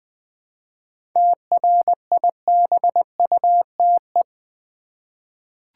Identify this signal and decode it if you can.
Morse code, heard as a keyed tone, about 20 wpm, 700 Hz. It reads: TRIBUTE
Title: 7 Letter Words - First 3 Letters Given - 20wpm